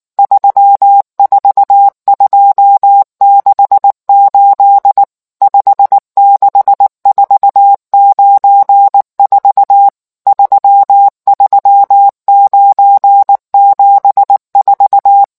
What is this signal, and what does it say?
Morse code, heard as a keyed tone, about 19 wpm, 800 Hz. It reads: 34268 56494 33974